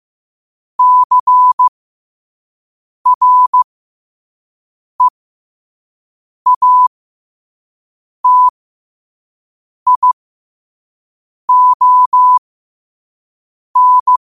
Morse code, heard as a keyed tone, about 15 wpm, spaced as for 5 wpm, 1000 Hz. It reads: CREATION
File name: qotdfiles / 5wpm